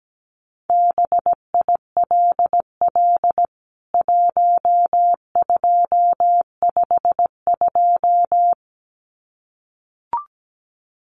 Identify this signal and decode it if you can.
Morse code, heard as a keyed tone, about 17 wpm, 700 Hz. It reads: BILL 1252
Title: CWT Exchange 17wpm